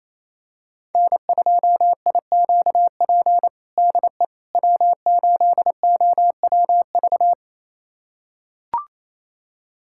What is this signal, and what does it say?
Morse code, heard as a keyed tone, about 28 wpm, 700 Hz. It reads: N2IQP DE W8OWV
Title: USA Callsign DE Callsign - 5 Characters - 28wpm